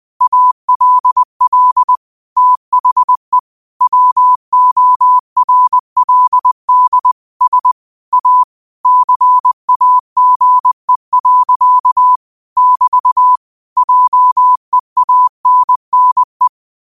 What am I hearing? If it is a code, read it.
Morse code, heard as a keyed tone, about 20 wpm, 1000 Hz. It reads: ALL THE WORLD S A CAGE. <BT> JEANNE
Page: Quotes for Sat, 16 Aug 2025 in Morse Code at 20 words per minute.